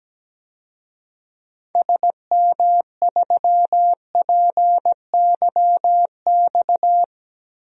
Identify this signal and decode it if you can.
Morse code, heard as a keyed tone, about 17 wpm, 700 Hz. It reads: SM3PYX
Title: DXCC-100 Call Signs - 6 Characters - 17wpm